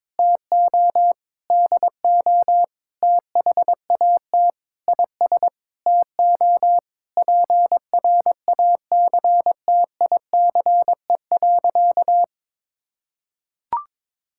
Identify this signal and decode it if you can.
Morse code, heard as a keyed tone, about 22 wpm, 700 Hz. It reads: TO DO THAT IS TO PRACTICE.
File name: Sentences with 9-Word Phrases - 22wpm